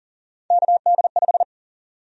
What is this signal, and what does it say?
Morse code, heard as a keyed tone, about 40 wpm, 700 Hz. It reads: KD5
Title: USA Call Sign Prefixes - 40wpm